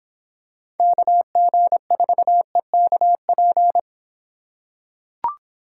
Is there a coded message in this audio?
Yes: Morse code from 0.8 to 3.8 s, heard as a keyed tone, about 26 wpm, 700 Hz. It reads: KG4EKP